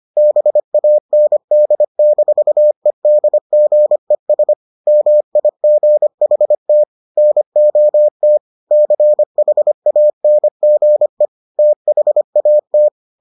Morse code, heard as a keyed tone, about 25 wpm, 600 Hz. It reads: BAND-EDGES MIGHT NOT CHANGE THAT